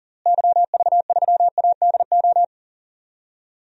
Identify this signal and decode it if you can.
Morse code, heard as a keyed tone, about 40 wpm, 700 Hz. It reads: YV3ADO
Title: DXCC-100 Call Signs - 6 Characters - Repeated 3x - 40wpm